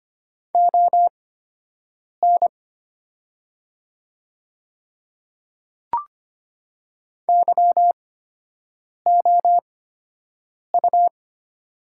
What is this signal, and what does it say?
Morse code, heard as a keyed tone, about 25 wpm, spaced as for 7 wpm, 700 Hz. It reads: ON YOU
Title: 8x Character Spacing - Top 100 Words - 25wpm